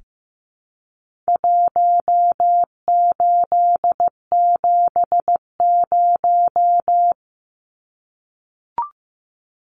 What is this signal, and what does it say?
Morse code, heard as a keyed tone, about 15 wpm, 700 Hz. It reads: 1870
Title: Numbers - Contest Style - 4-Digit 15wpm